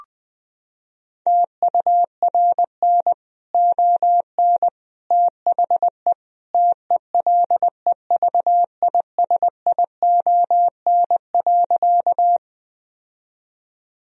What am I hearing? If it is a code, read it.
Morse code, heard as a keyed tone, about 20 wpm, 700 Hz. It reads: TURN ON THE TELEVISION.
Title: Sentences from Top 1000 Words - Review Part 4 - 20wpm